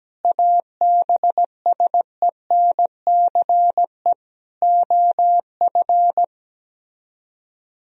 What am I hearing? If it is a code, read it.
Morse code, heard as a keyed tone, about 17 wpm, 700 Hz. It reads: ABSENCE OF
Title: Speed-Racing - Sets of 4 Words 17wpm